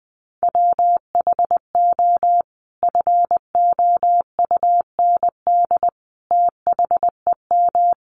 Morse code, heard as a keyed tone, about 19 wpm, 700 Hz. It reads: WHO FOUND THEM
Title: Sentences from Top 300 Words 20wpm